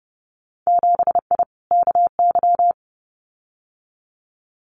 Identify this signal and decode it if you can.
Morse code, heard as a keyed tone, about 30 wpm, 700 Hz. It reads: 7I KY
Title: Winter Field Day Exchanges Encore 30wpm